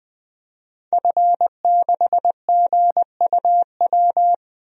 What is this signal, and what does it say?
Morse code, heard as a keyed tone, about 20 wpm, 700 Hz. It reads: F6GUW
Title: DXCC-100 Call Signs - 5 Characters - Repeated 2x - 20wpm